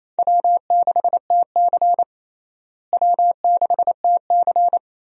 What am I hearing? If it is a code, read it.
Morse code, heard as a keyed tone, about 27 wpm, 700 Hz. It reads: W6TC W6TC
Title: USA Call Signs - 4 Characters - Repeated 2x - 28wpm